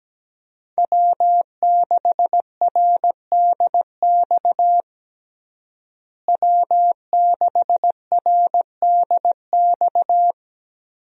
Morse code, heard as a keyed tone, about 17 wpm, 700 Hz. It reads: W6RDX W6RDX
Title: USA Call Signs - 5 Characters - Repeated 2x - 17wpm